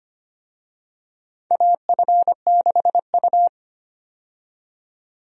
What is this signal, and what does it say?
Morse code, heard as a keyed tone, about 25 wpm, 700 Hz. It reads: AF6U
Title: USA Call Signs - 4 Characters - 25wpm